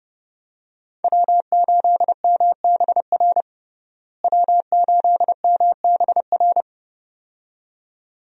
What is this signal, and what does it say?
Morse code, heard as a keyed tone, about 30 wpm, 700 Hz. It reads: W8MBR W8MBR
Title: USA Call Signs - 5 Characters - Repeated 2x - 30wpm